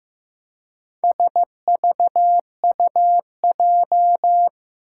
Morse code, heard as a keyed tone, about 15 wpm, 700 Hz. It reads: SVUJ
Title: Four Challenging Letter-Number Combinations 15wpm